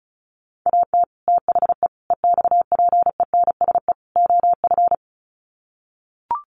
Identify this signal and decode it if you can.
Morse code, heard as a keyed tone, about 34 wpm, 700 Hz. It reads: AT THE EXPENSE OF
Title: Sets of 4 Words 35wpm